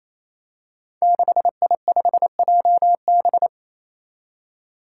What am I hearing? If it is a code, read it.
Morse code, heard as a keyed tone, about 28 wpm, 700 Hz. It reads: 6I5JB
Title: Five Challenging Letter-Number Combinations 28wpm